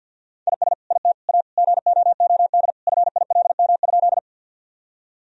Morse code, heard as a keyed tone, about 50 wpm, 700 Hz. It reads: IS IT A GOOD FILM?
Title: Sentences from Top 1000 Words - Review Part 3 - 50wpm